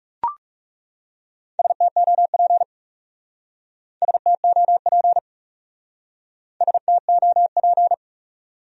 Morse code, heard as a keyed tone, about 37 wpm, 700 Hz. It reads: STOP STOP STOP
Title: Speed-Racing Top 500 Words 30wpm